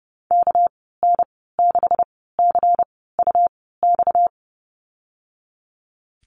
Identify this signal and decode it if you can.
Morse code, heard as a keyed tone, about 30 wpm, spaced as for 17 wpm, 700 Hz. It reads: KN6CUX
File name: Lesson 37g - Letter x - Call Signs 30wpm